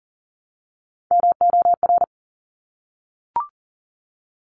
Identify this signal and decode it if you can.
Morse code, heard as a keyed tone, about 40 wpm, 700 Hz. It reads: MOR
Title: Top Three Letter Combinations in Words 40wpm